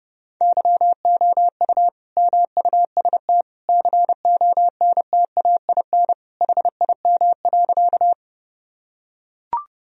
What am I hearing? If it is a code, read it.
Morse code, heard as a keyed tone, about 32 wpm, 700 Hz. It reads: YOU MUST CONTAIN HIM.
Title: Sentences from Top 1000 Words - Review Part 2 - 30wpm